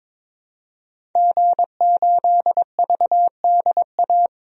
Morse code, heard as a keyed tone, about 22 wpm, 700 Hz. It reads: G8VDA
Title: DXCC-100 Call Signs - 5 Characters - 22wpm